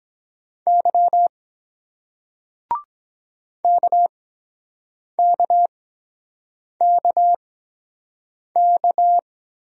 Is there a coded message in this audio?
Yes: Morse code from 0.7 to 9.2 s, heard as a keyed tone, about 23 wpm, spaced as for 7 wpm, 700 Hz. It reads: Y KKKK